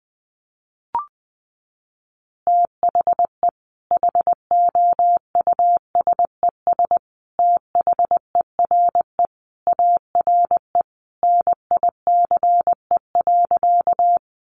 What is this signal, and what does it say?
Morse code, heard as a keyed tone, about 19 wpm, 700 Hz. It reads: THE HOUSES THERE ARE NICE.
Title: Easy Phrases 20wpm